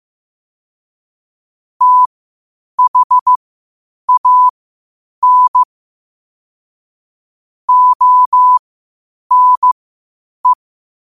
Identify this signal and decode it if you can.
Morse code, heard as a keyed tone, about 15 wpm, spaced as for 8 wpm, 1000 Hz. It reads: THAN ONE